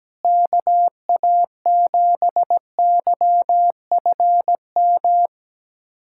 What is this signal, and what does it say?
Morse code, heard as a keyed tone, about 17 wpm, 700 Hz. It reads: KA7YFM
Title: USA Callsign DE Callsign - 6 Characters - 17wpm